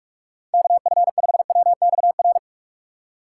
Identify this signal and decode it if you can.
Morse code, heard as a keyed tone, about 46 wpm, 700 Hz. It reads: KF5WXR
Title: USA Call Signs - 6 Characters - 4x Character Spacing - 45wpm